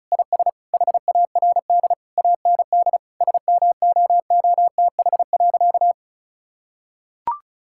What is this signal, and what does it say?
Morse code, heard as a keyed tone, about 34 wpm, 700 Hz. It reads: IS HARD AND SMOOTH.
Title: Sentences from Top 1600 Words 35wpm